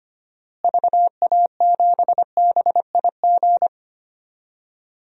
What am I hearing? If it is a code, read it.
Morse code, heard as a keyed tone, about 25 wpm, 700 Hz. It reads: VA7BIG
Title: DXCC-100 Call Signs - 6 Characters - Repeated 3x - 25wpm